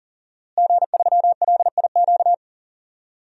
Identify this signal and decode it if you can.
Morse code, heard as a keyed tone, about 40 wpm, 700 Hz. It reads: G3LIQ
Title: DXCC-100 Call Signs - 5 Characters - Repeated 3x - 40wpm